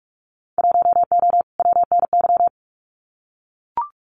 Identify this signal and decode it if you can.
Morse code, heard as a keyed tone, about 44 wpm, 700 Hz. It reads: 1O WNY
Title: Winter Field Day Exchanges Encore 45wpm